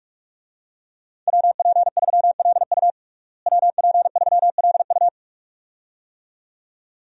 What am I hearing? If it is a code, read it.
Morse code, heard as a keyed tone, about 46 wpm, 700 Hz. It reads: WP3LU WP3LU
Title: USA Call Signs - 5 Characters - Repeated 2x - 45wpm